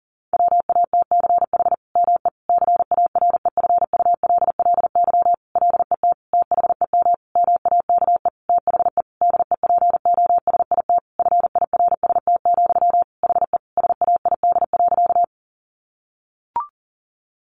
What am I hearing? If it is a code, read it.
Morse code, heard as a keyed tone, about 37 wpm, 700 Hz. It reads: WATCH ME CAREFULLY LET THEM MAKE THE DEPOSIT FIRST, HE SAID.
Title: Sentences from Top 1300 Words 40wpm